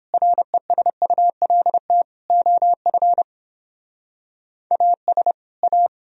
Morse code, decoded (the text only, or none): RESULT OF AS A